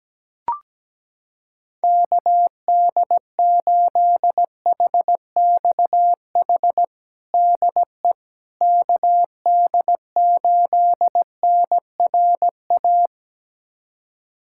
Morse code, heard as a keyed tone, about 17 wpm, 700 Hz. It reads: KD8HXH DE KD8NRA
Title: USA Callsign DE Callsign 17wpm